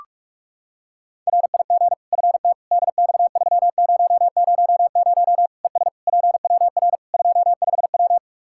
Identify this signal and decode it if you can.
Morse code, heard as a keyed tone, about 46 wpm, 700 Hz. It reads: RIG FT DX3000 ES PWR 25W
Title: Practice QSO - Exchange 2 - Info - 45wpm